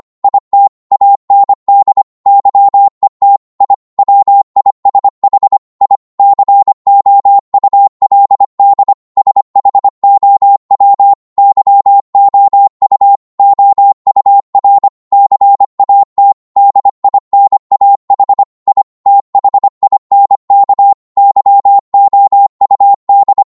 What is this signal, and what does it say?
Morse code, heard as a keyed tone, about 25 wpm, 800 Hz. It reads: IT AND YET I WISH I COULD SHOW YOU OUR CAT DINAH I THINK YOUD